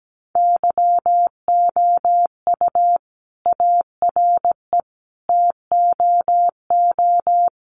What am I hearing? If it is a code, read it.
Morse code, heard as a keyed tone, about 17 wpm, 700 Hz. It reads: YOU ARE TOO
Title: Sentences from Top 1400 Words 17wpm